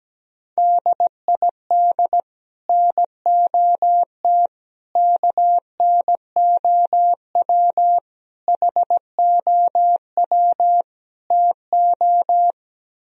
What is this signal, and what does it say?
Morse code, heard as a keyed tone, about 17 wpm, 700 Hz. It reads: DID NOT KNOW HOW TO
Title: Speed-Racing - Sets of 5 Words 17wpm